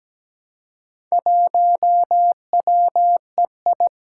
Morse code, heard as a keyed tone, about 17 wpm, 700 Hz. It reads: 1WEI
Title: Four Challenging Letter-Number Combinations 17wpm